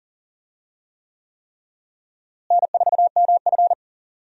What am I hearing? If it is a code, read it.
Morse code, heard as a keyed tone, about 40 wpm, 700 Hz. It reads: N4MF
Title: USA Call Signs - 4 Characters - Repeated 3x - 40wpm